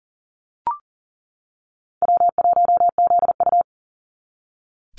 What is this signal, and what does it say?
Morse code, heard as a keyed tone, about 40 wpm, 700 Hz. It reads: W1ZU
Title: NA SOTA Activator Callsigns 40wpm